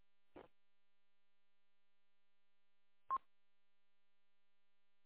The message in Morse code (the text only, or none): none